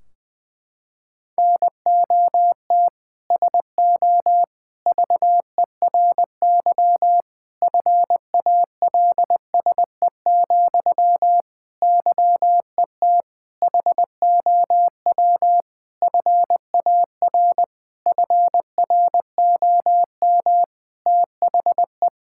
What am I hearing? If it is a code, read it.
Morse code, heard as a keyed tone, about 20 wpm, 700 Hz. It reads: NOT SO VERY FALSE, YET HOW FAR FROM THE
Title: Sentences from Top 1600 Words 20wpm